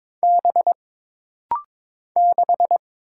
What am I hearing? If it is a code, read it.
Morse code, heard as a keyed tone, about 23 wpm, 700 Hz. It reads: B 6
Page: This rapid-fire timing is geared to push your ICR (Instant Character Recognition) skills.